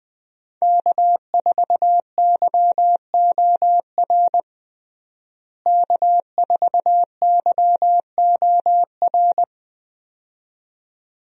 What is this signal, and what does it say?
Morse code, heard as a keyed tone, about 20 wpm, 700 Hz. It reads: K4YOR K4YOR
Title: USA Call Signs - 5 Characters - Repeated 2x - 20wpm